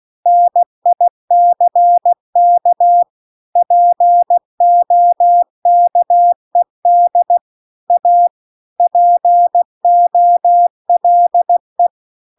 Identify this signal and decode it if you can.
Morse code, heard as a keyed tone, about 16 wpm, 700 Hz. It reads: NICK POKED A POLE